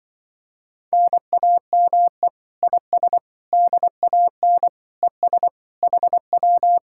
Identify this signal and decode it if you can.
Morse code, heard as a keyed tone, about 24 wpm, 700 Hz. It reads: NAME IS DAN ES HW
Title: Speed-Racing Common QSO Phrases - 20wpm